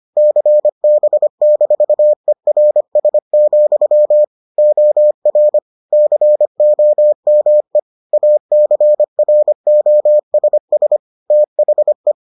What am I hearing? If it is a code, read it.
Morse code, heard as a keyed tone, about 25 wpm, 600 Hz. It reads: CB-ERS, OR COME ACROSS THE